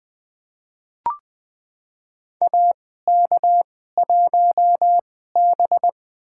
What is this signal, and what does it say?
Morse code, heard as a keyed tone, about 20 wpm, spaced as for 14 wpm, 700 Hz. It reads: AK1B
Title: USA Call Signs - 2x Character Spacing - 20wpm